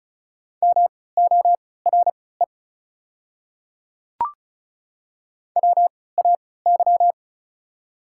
Morse code, heard as a keyed tone, about 34 wpm, spaced as for 20 wpm, 700 Hz. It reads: MORE WAY
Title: 3x Character Spacing - Top 200 Words - 35wpm